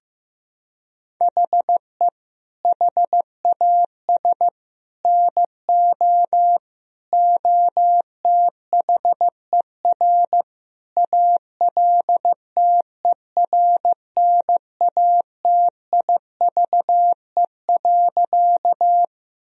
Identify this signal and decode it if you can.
Morse code, heard as a keyed tone, about 15 wpm, 700 Hz. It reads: HE HAS NO OTHER ALTERNATIVE.
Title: Sentences from Top 1000 Words - Review Part 4 - 15wpm